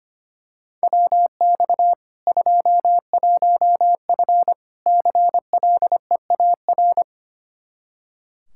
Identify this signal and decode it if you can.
Morse code, heard as a keyed tone, about 25 wpm, 700 Hz. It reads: WX 21F CLEAR
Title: K1USN SST QSO Practice 25wpm